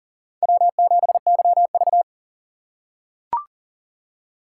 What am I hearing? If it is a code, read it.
Morse code, heard as a keyed tone, about 40 wpm, 700 Hz. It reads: W7YV
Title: USA Call Signs - 4 Characters - 2x Character Spacing - 40wpm